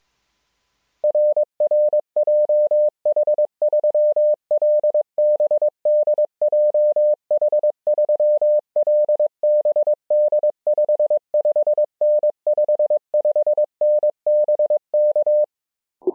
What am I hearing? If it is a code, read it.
Morse code, heard as a keyed tone, about 22 wpm, 600 Hz. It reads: RR JH3LBD JH3LBD 55N 55N BK